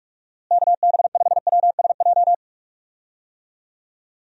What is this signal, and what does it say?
Morse code, heard as a keyed tone, about 43 wpm, 700 Hz. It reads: KB5WSJ
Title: USA Call Signs - 6 Characters - Repeated 3x - 45wpm